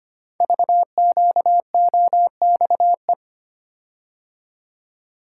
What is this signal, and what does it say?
Morse code, heard as a keyed tone, about 25 wpm, 700 Hz. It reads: VQOXE
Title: Code Groups - 5 Letters - 25wpm